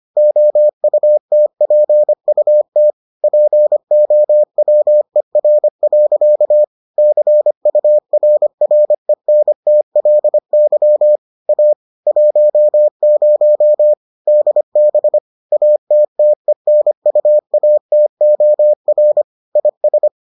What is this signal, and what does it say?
Morse code, heard as a keyed tone, about 25 wpm, 600 Hz. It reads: OUTPUT POWER. CURRENTLY A 10 DB ATTENUATOR IS